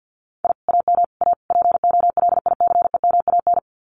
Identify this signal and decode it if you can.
Morse code, heard as a keyed tone, about 50 wpm, 700 Hz. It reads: I AM A POLICEMAN
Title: Common QSO Phrases Encore 50wpm